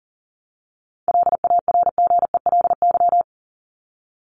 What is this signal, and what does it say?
Morse code, heard as a keyed tone, about 40 wpm, 700 Hz. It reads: LARGELY
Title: LY Suffix - Top 100 Words with Base Words 40wpm